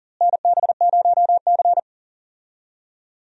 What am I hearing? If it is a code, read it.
Morse code, heard as a keyed tone, about 38 wpm, 700 Hz. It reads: NB0C
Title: USA Call Signs - 4 Characters - Repeated 3x - 40wpm